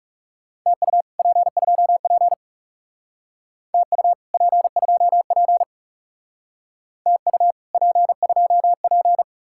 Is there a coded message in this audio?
Yes: Morse code from 0.7 to 9.2 s, heard as a keyed tone, about 38 wpm, 700 Hz. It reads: TU P2P TU P2P TU P2P